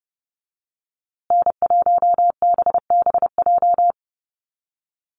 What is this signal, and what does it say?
Morse code, heard as a keyed tone, about 29 wpm, 700 Hz. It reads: N1BBJ
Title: USA Call Signs Repeated 3x 30wpm